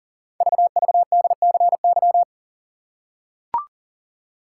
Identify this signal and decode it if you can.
Morse code, heard as a keyed tone, about 40 wpm, 700 Hz. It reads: VVDCY
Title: Code Groups - 5 Letters - 40wpm